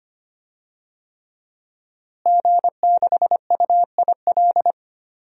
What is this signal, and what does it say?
Morse code, heard as a keyed tone, about 25 wpm, 700 Hz. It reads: G6UIL
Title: DXCC-100 Call Signs - 5 Characters - Repeated 3x - 25wpm